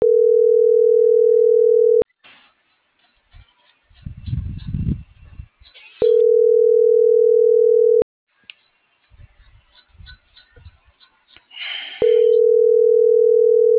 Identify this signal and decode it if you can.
Ringback tone (North America), 440 + 480 Hz.